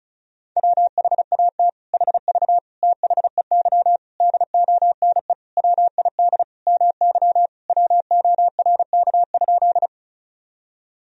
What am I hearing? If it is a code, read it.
Morse code, heard as a keyed tone, about 34 wpm, 700 Hz. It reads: WHAT HV THEY DONE WID MY WORK?
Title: Sentences from Top 500 Words and Abbreviations 35wpm